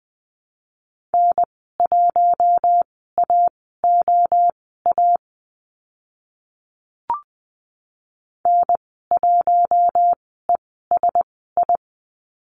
Morse code, heard as a keyed tone, about 19 wpm, spaced as for 14 wpm, 700 Hz. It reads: N1AOA N1ESI
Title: 042 Lesson 8g - Number 1 - Call Signs 20wpm